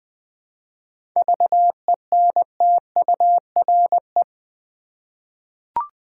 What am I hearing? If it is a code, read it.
Morse code, heard as a keyed tone, about 20 wpm, 700 Hz. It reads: VENTURE
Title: 7 Letter Words - First 3 Letters Given - 20wpm